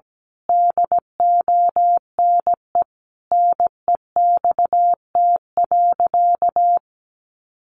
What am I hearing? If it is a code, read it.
Morse code, heard as a keyed tone, about 17 wpm, 700 Hz. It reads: DONE NEXT.
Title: Sentences from Top 1200 Words 17wpm